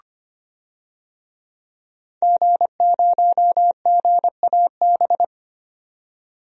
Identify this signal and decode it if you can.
Morse code, heard as a keyed tone, about 25 wpm, 700 Hz. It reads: G0GAB
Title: DXCC-100 Call Signs - 5 Characters - 25wpm